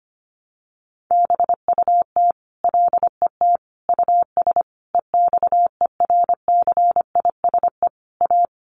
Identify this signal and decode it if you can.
Morse code, heard as a keyed tone, about 25 wpm, 700 Hz. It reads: BUT LET US EXERCISE A
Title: Sentences from Top 1300 Words 25wpm